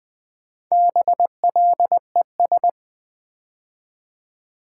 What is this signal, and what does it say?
Morse code, heard as a keyed tone, about 20 wpm, 700 Hz. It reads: BLES
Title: Build A Word - 4 Letters At A Time - 8-Letter Words - 20wpm